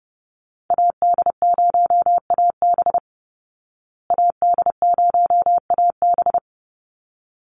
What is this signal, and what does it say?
Morse code, heard as a keyed tone, about 30 wpm, 700 Hz. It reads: AD0AB AD0AB
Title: Call Signs Repeated 3x - 30wpm